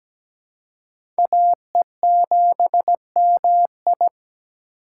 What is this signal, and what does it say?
Morse code, heard as a keyed tone, about 17 wpm, 700 Hz. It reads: AE7MI
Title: USA Call Signs - 5 Characters - 3x Character Spacing - 17wpm